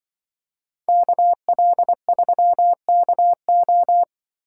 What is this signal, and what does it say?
Morse code, heard as a keyed tone, about 24 wpm, 700 Hz. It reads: KL3KO
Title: Speed Racing - USA 2x2 Call Signs 20wpm